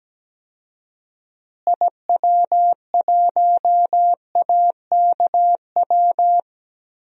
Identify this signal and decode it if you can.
Morse code, heard as a keyed tone, about 17 wpm, 700 Hz. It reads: IW1AKW
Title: DXCC-100 Call Signs - 6 Characters - Repeated 3x - 17wpm